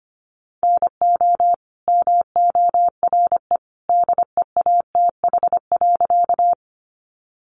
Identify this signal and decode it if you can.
Morse code, heard as a keyed tone, about 25 wpm, 700 Hz. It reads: NO MORE DEATH.
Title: Sentences from Top 1100 Words 25wpm